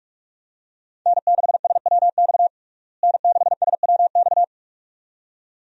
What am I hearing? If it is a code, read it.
Morse code, heard as a keyed tone, about 46 wpm, 700 Hz. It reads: N6SWX N6SWX
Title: USA Call Signs - 5 Characters - Repeated 2x - 45wpm